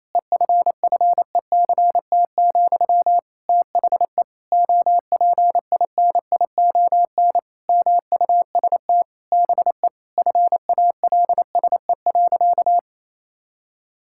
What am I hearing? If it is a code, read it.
Morse code, heard as a keyed tone, about 28 wpm, 700 Hz. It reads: EFFECT, THE OPINION MUST BE FALSE.
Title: Sentences from Top 1600 Words 28wpm